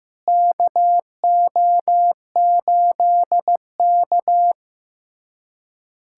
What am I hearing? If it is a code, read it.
Morse code, heard as a keyed tone, about 15 wpm, 700 Hz. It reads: KO8K
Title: USA Callsign DE Callsign - 4 Characters - 15wpm